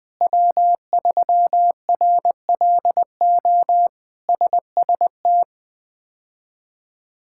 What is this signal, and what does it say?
Morse code, heard as a keyed tone, about 20 wpm, 700 Hz. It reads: W3RLO SST
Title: K1USN SST QSO Practice - 20wpm